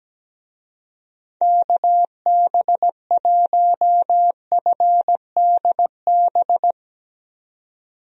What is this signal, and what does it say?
Morse code, heard as a keyed tone, about 17 wpm, 700 Hz. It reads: KB1FDB
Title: USA Call Signs - 6 Characters - 17wpm